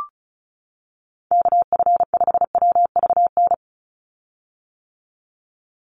Morse code, heard as a keyed tone, about 36 wpm, 700 Hz. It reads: KF5WVN
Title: USA Call Signs 35wpm